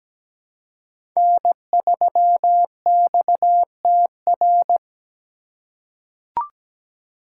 Morse code, heard as a keyed tone, about 17 wpm, 700 Hz. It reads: N3XTR
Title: USA Call Signs - 5 Characters - 3x Character Spacing - 17wpm